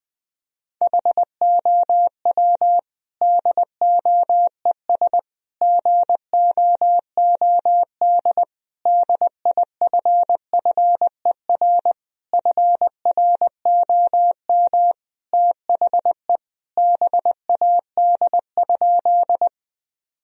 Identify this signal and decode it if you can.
Morse code, heard as a keyed tone, about 21 wpm, 700 Hz. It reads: HOW DOES GOOD DIFFER FROM THE BAD?
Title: Sentences from Top 1000 Words - Review Part 1 - 20wpm